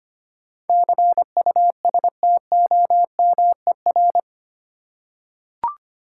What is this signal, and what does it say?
Morse code, heard as a keyed tone, about 25 wpm, 690 Hz. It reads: CUSTOMER
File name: 8 Letter Words - First 3 Letters Given - 25wpm